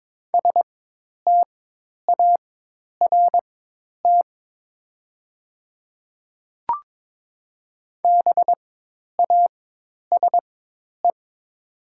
Morse code, heard as a keyed tone, about 22 wpm, spaced as for 10 wpm, 700 Hz. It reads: START BASE
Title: 4x Character Spacing - Top 300 Words - 22wpm